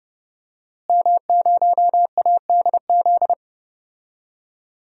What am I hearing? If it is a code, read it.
Morse code, heard as a keyed tone, about 30 wpm, 700 Hz. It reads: M0ADZ
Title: DXCC-100 Call Signs - 5 Characters - Repeated 3x - 30wpm